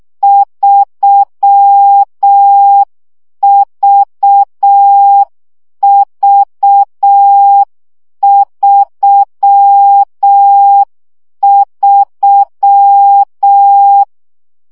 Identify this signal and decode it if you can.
Morse code, heard as a keyed tone, about 6 wpm, 800 Hz. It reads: 3VV33
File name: CW gyakorlas v 3 5 baud